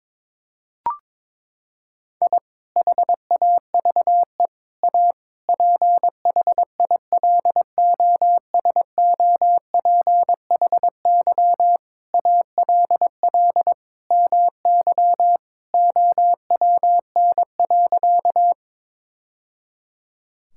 Morse code, heard as a keyed tone, about 23 wpm, 700 Hz. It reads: I HAVE A PHILOSOPHY ALL MY OWN.
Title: Sentences from Top 1000 Words - Review Part 6 - 22wpm